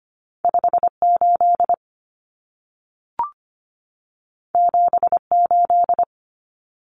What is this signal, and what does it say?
Morse code, heard as a keyed tone, about 25 wpm, 700 Hz. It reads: 58 78
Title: Common Cut Numbers - 2-Digit 25wpm